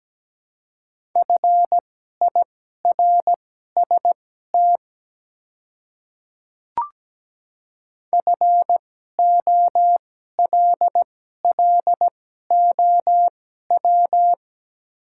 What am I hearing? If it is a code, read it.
Morse code, heard as a keyed tone, about 17 wpm, spaced as for 12 wpm, 700 Hz. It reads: FIRST FOLLOW
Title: 2x Character Spacing - Top 300 Words - 17wpm